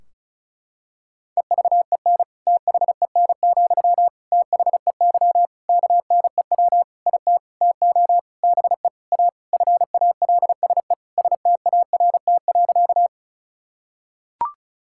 Morse code, heard as a keyed tone, about 36 wpm, 700 Hz. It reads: EVEN THEN, THEY KNEW IT TO BE A FALSE START.
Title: Sentences from Top 1600 Words 35wpm